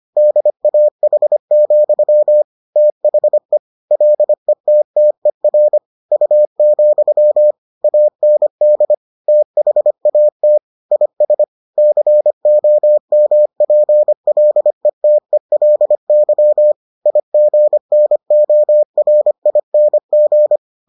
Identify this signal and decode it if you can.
Morse code, heard as a keyed tone, about 25 wpm, 600 Hz. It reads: DAH, THE LETTER U, AND THAT IS COMPLETELY IGNORING